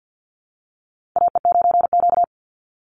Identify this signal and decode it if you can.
Morse code, heard as a keyed tone, about 48 wpm, 700 Hz. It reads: AE9Q
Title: NA SOTA Chaser Callsigns 50wpm